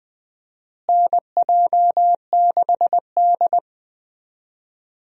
Morse code, heard as a keyed tone, about 21 wpm, 700 Hz. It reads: NJ6D
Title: USA Call Signs - 4 Characters - Repeated 2x - 20wpm